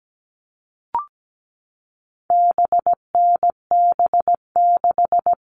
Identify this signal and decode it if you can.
Morse code, heard as a keyed tone, about 17 wpm, 700 Hz. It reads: BNB6
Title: All Cut Numbers - 4-Digit 17wpm